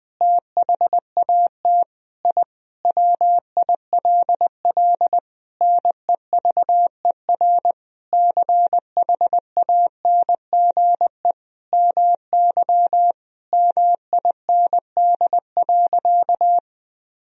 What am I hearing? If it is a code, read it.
Morse code, heard as a keyed tone, about 20 wpm, 700 Hz. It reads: THAT I WILL NEVER CHANGE MY MIND.
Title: Sentences from Top 1000 Words - Review Part 1 - 20wpm